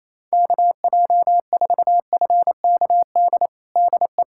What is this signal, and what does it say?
Morse code, heard as a keyed tone, about 28 wpm, 700 Hz. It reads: KJ4FKD DE
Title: USA Callsign DE Callsign 28wpm